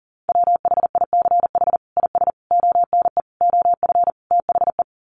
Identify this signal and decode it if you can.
Morse code, heard as a keyed tone, about 40 wpm, 700 Hz. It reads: WHICH IS ONE OF THE
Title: Sets of 5 Words Encore 40wpm